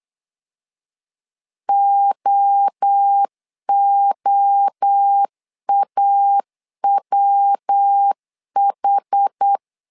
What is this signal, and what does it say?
Morse code, heard as a keyed tone, about 8 wpm, 800 Hz. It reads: OOAWH
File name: 4 Q 8wpm